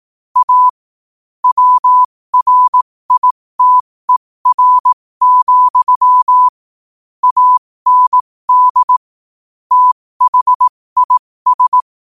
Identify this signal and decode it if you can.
Morse code, heard as a keyed tone, about 18 wpm, spaced as for 15 wpm, 1000 Hz. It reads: A WRITER, AND THIS
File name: qotdfiles / 15wpm